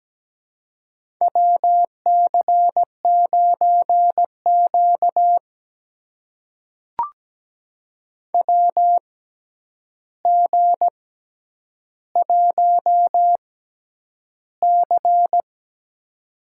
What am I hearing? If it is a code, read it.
Morse code, heard as a keyed tone, about 17 wpm, spaced as for 9 wpm, 700 Hz. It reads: WC9Q WG1C
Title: USA Call Signs - 4 Characters - 6x Character Spacing - 17wpm